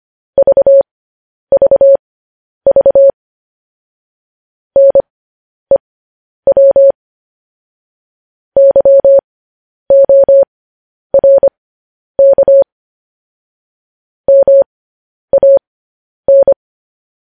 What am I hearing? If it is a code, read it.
Morse code, heard as a keyed tone, about 25 wpm, spaced as for 10 wpm, 550 Hz. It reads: VVV NEW YORK MAN
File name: podcast / News headlines-1x-25x10